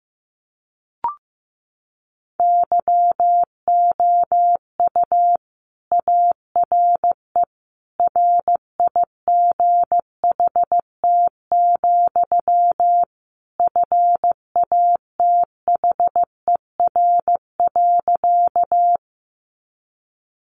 Morse code, heard as a keyed tone, about 15 wpm, 700 Hz. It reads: YOU ARE RIGHT, FATHER.
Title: Sentences from Top 300 Words 15wpm